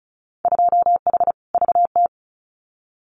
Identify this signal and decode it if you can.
Morse code, heard as a keyed tone, about 34 wpm, 700 Hz. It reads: 2H VT
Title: Winter Field Day Exchanges 35wpm